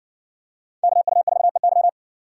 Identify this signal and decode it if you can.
Morse code, heard as a keyed tone, about 60 wpm, 700 Hz. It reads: KU4EY